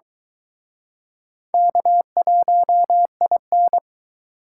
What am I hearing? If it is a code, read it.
Morse code, heard as a keyed tone, about 22 wpm, 700 Hz. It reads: K1IN